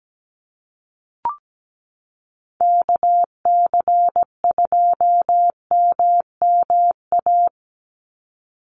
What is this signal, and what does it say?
Morse code, heard as a keyed tone, about 17 wpm, 700 Hz. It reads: KC2MMA
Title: USA Call Signs 17wpm